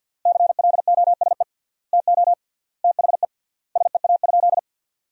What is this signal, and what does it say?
Morse code, heard as a keyed tone, about 48 wpm, 700 Hz. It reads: CLOSE TO THE SEA?
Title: 3x Word-Spacing - Sentences from Top 300 Words - 50wpm